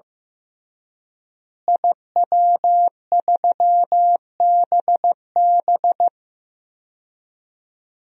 Morse code, heard as a keyed tone, about 15 wpm, 700 Hz. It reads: IW3BB
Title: DXCC-100 Call Signs - 5 Characters - Repeated 3x - 15wpm